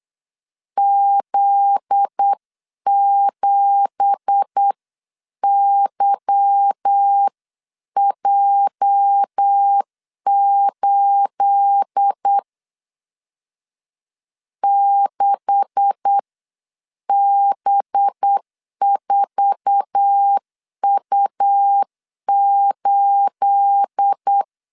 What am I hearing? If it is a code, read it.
Morse code, heard as a keyed tone, about 8 wpm, spaced as for 7 wpm, 800 Hz. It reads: Z7YJ8 6B4U8
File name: CW / DET / 1 9 1500C S A 8wpm 5637